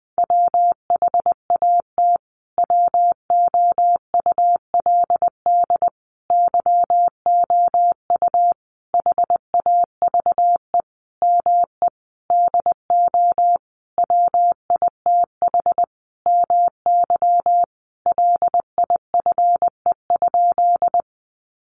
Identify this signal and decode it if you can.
Morse code, heard as a keyed tone, about 20 wpm, 700 Hz. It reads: WHAT WOULD YOU HAVE ME DO WITH MY LIFE?
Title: Sentences from Top 300 Words 20wpm